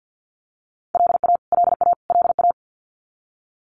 Morse code, heard as a keyed tone, about 50 wpm, 700 Hz. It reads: LA LA LA
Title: US States Repeated 3x 50wpm